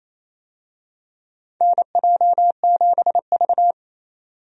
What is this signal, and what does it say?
Morse code, heard as a keyed tone, about 28 wpm, 700 Hz. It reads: NJ7V
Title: USA Call Signs - 4 Characters - 28wpm